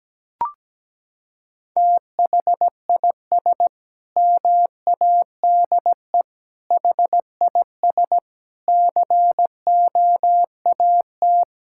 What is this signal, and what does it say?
Morse code, heard as a keyed tone, about 17 wpm, 700 Hz. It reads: THIS MADE HIS COAT
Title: Sentences from Top 1600 Words 17wpm